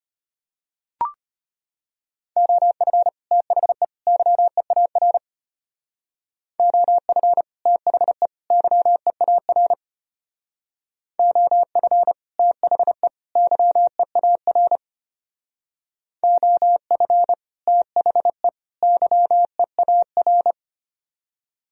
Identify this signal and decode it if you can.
Morse code, heard as a keyed tone, about 32 wpm, 700 Hz. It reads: OF THE YEAR OF THE YEAR OF THE YEAR OF THE YEAR